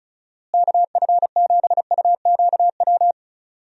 Morse code, heard as a keyed tone, about 36 wpm, 700 Hz. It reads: KF7UQW
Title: USA Call Signs - 6 Characters - Repeated 2x - 35wpm